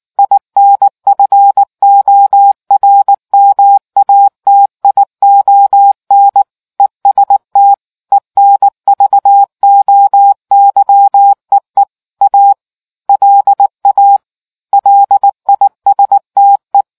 Morse code, heard as a keyed tone, about 19 wpm, 800 Hz. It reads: INFORMATION EST ENVOYEE A LA LISTE